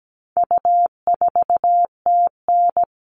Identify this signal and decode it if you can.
Morse code, heard as a keyed tone, about 17 wpm, 700 Hz. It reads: U4TN
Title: All Cut Numbers - 4-Digit 17wpm